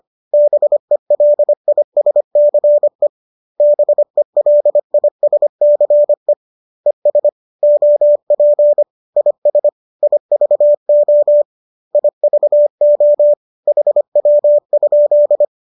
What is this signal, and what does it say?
Morse code, heard as a keyed tone, about 25 wpm, 590 Hz. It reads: BELISCE BELISCE ES OP IS IVO IVO HW?
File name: qso219 25